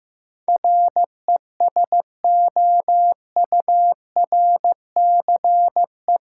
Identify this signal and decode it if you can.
Morse code, heard as a keyed tone, about 15 wpm, 700 Hz. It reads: RESOURCE
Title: Sentences from Top 1300 Words - New Word Drill Alt - 15wpm